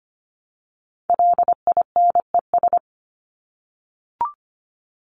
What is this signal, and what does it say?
Morse code, heard as a keyed tone, about 25 wpm, 700 Hz. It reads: LINES
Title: ES:S Suffix - Top 100 Words with Base Words 25wpm